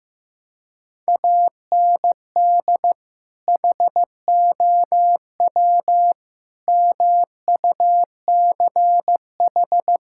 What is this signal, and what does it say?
Morse code, heard as a keyed tone, about 15 wpm, 700 Hz. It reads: AND HOW MUCH
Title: Sentences from Top 1000 Words - Review Part 5 - 15wpm